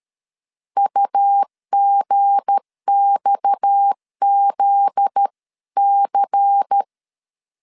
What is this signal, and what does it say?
Morse code, heard as a keyed tone, about 13 wpm, 800 Hz. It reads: UGXZC
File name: CW / DET / 1 9 1500C S C 12wpm 3811